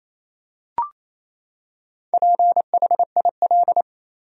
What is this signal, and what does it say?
Morse code, heard as a keyed tone, about 28 wpm, 700 Hz. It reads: PHIL
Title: USA Common Names - Max 4 Letters - 28wpm